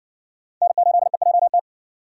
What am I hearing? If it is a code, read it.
Morse code, heard as a keyed tone, about 53 wpm, 700 Hz. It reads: N8EJT